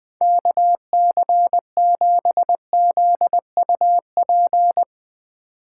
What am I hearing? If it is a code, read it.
Morse code, heard as a keyed tone, about 21 wpm, 700 Hz. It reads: KC7ZUP
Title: Speed Racing - USA 2x3 Call Signs 20wpm